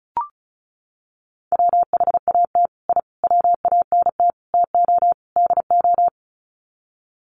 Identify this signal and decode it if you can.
Morse code, heard as a keyed tone, about 34 wpm, 700 Hz. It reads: WHAT I WANT TO DO
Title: Sets of 5 Words Encore 35wpm